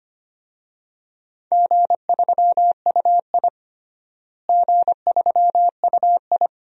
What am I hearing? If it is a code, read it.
Morse code, heard as a keyed tone, about 25 wpm, 700 Hz. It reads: G3UI G3UI
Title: DXCC-100 Call Signs - 4 Characters - Repeated 2x - 25wpm